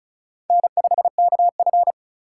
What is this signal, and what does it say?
Morse code, heard as a keyed tone, about 36 wpm, 700 Hz. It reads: N5KF
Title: USA Call Signs - 6x Character Spacing - 35wpm